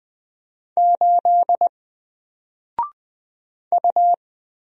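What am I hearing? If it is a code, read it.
Morse code, heard as a keyed tone, about 20 wpm, 700 Hz. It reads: 8 U
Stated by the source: Single letter and number Morse drill at 20 wpm, from N4LCD.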